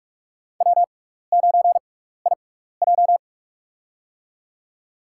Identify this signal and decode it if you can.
Morse code, heard as a keyed tone, about 42 wpm, spaced as for 15 wpm, 700 Hz. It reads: W9IJ
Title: USA Call Signs - 6x Character Spacing - 45wpm